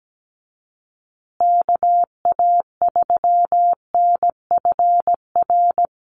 Morse code, heard as a keyed tone, about 17 wpm, 700 Hz. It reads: KA3NFR
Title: USA Call Signs Repeated 3x 17wpm